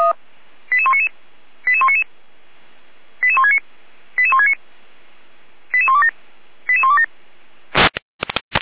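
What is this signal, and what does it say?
Five-tone selective call (ZVEI), heard as quick runs of five tones, tones of about 70 ms. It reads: 88199 88199 88169 88169 88127 88127
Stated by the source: ZVEI five-tone selective call received by SDR, from the feuerwehr-orlamuende.org fire brigade audio archive.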